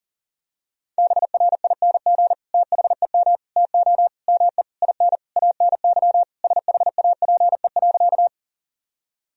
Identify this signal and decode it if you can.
Morse code, heard as a keyed tone, about 40 wpm, 700 Hz. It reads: BRING THEM TO ME IN ANY SHAPE.
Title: Sentences from Top 1000 Words - Review Part 3 - 40wpm